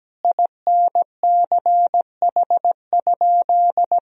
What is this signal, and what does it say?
Morse code, heard as a keyed tone, about 17 wpm, 700 Hz. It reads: INCH?
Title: Sentences from Top 1000 Words - Review Part 3 - 17wpm